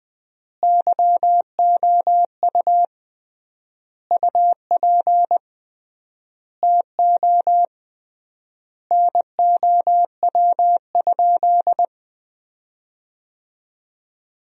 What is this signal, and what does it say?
Morse code, heard as a keyed tone, about 20 wpm, 700 Hz. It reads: YOU UP TO NOW?
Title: 3x Word-Spacing - Sentences from Top 100 Words - 20wpm